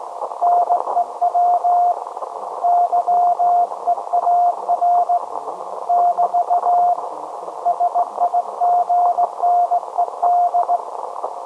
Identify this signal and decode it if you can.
Morse code, heard as a keyed tone, about 18 wpm, 710 Hz. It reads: NEW YEAR <BT> SIGNED
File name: storage / message Grimeton SAQ